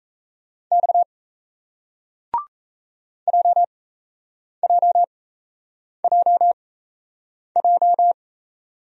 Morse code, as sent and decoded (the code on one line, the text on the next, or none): -..- / .--- .--- .--- .---
X JJJJ